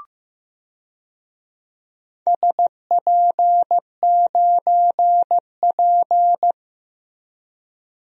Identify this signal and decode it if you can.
Morse code, heard as a keyed tone, about 15 wpm, 700 Hz. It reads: SP9P
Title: DXCC-100 Call Signs - 4 Characters - Repeated 2x - 15wpm